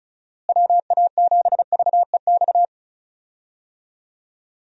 Morse code, heard as a keyed tone, about 36 wpm, 700 Hz. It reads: WA7VEX
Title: USA Call Signs - 6 Characters - Repeated 2x - 35wpm